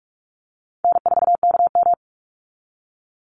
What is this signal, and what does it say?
Morse code, heard as a keyed tone, about 46 wpm, 700 Hz. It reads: N4KM
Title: Easy Call Signs - 45wpm